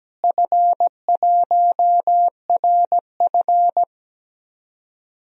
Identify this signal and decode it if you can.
Morse code, heard as a keyed tone, about 17 wpm, 700 Hz. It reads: F1RF
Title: DXCC-100 Call Signs - 4 Characters - Repeated 3x - 17wpm